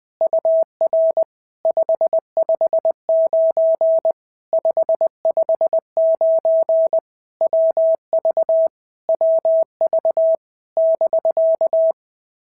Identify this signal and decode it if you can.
Morse code, heard as a keyed tone, about 20 wpm, 650 Hz. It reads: UR 559 559 WV WV <BK>
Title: POTA QSO Practice - 20wpm